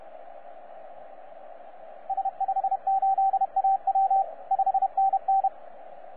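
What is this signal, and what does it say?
Morse code, heard as a keyed tone, about 31 wpm, 780 Hz, with radio noise. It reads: S58AW 5NN
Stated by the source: 9U4U on 15 metre CW, DX station recording by DX World.net.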